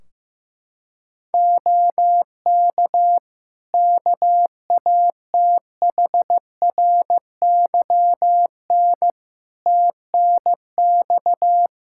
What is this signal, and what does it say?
Morse code, heard as a keyed tone, about 15 wpm, 700 Hz. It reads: OK KATHRYN TNX
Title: K1USN SST QSO Practice 15wpm